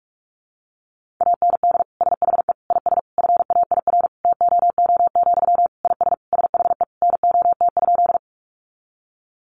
Morse code, heard as a keyed tone, about 44 wpm, 700 Hz. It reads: AND SHE IS FAIR TOO, IS SHE NOT?
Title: Sentences from Top 1300 Words 45wpm